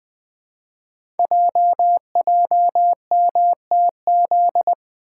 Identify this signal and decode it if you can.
Morse code, heard as a keyed tone, about 20 wpm, 700 Hz. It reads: JJMTZ
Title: Code Groups - 5 Letters - 20wpm